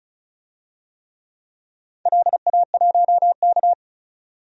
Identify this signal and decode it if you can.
Morse code, heard as a keyed tone, about 36 wpm, 700 Hz. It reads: LA1K
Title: DXCC-100 Call Signs - 4 Characters - 35wpm